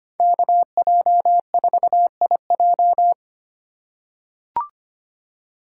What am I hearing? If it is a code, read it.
Morse code, heard as a keyed tone, about 25 wpm, 700 Hz. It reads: KJ4IJ
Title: USA Call Signs - 5 Characters - 3x Character Spacing - 25wpm